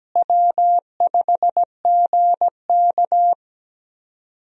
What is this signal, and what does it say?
Morse code, heard as a keyed tone, about 17 wpm, 700 Hz. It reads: W5GK
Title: USA Call Signs - 4 Characters - Repeated 3x - 17wpm